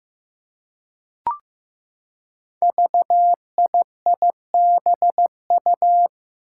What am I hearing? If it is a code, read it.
Morse code, heard as a keyed tone, about 15 wpm, 700 Hz. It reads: VIIBU
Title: Five Challenging Letter-Number Combinations 15wpm